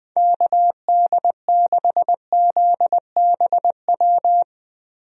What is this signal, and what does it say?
Morse code, heard as a keyed tone, about 20 wpm, 700 Hz. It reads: KD6ZBW
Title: USA Call Signs - 6 Characters - 20wpm